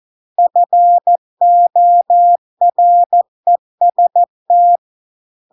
Morse code, heard as a keyed tone, about 14 wpm, 700 Hz. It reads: FOREST